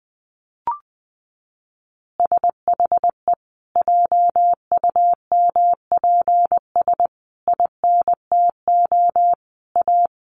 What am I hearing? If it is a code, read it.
Morse code, heard as a keyed tone, about 20 wpm, 700 Hz. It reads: SHE JUMPS INTO A
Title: Additional Easy Phrases 20wpm